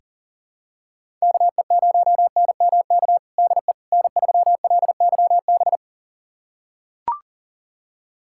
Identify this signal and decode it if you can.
Morse code, heard as a keyed tone, about 40 wpm, 700 Hz. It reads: KE0NMK DE N3LYB
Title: USA Callsign DE Callsign 40wpm